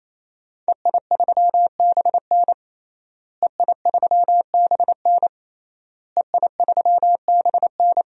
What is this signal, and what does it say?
Morse code, heard as a keyed tone, about 28 wpm, 700 Hz. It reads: EI3BN EI3BN EI3BN
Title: DXCC-100 Call Signs - 5 Characters - Repeated 3x - 28wpm